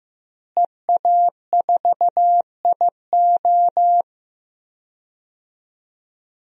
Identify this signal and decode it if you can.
Morse code, heard as a keyed tone, about 15 wpm, 700 Hz. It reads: EA4IO
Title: DXCC-100 Call Signs - 5 Characters - Repeated 2x - 15wpm